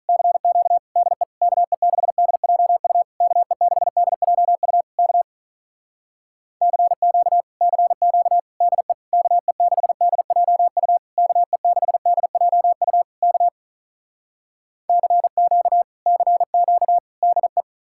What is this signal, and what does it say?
Morse code, heard as a keyed tone, about 46 wpm, 700 Hz. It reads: CQ DE KE6DJU KE6DJU K CQ CQ DE KE6DJU KE6DJU K CQ CQ DE